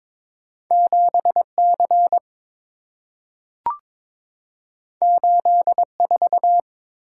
Morse code, heard as a keyed tone, about 22 wpm, 690 Hz. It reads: 7C 84
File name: practice / 22wpm / Two Letter-Number Combinations - 22wpm